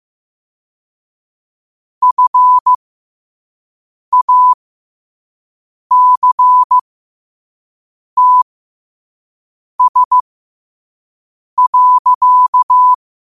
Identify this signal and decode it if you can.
Morse code, heard as a keyed tone, about 15 wpm, spaced as for 5 wpm, 1000 Hz. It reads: FACTS.
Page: Quotes for Wed, 13 Aug 2025 in Morse Code at 5 words per minute.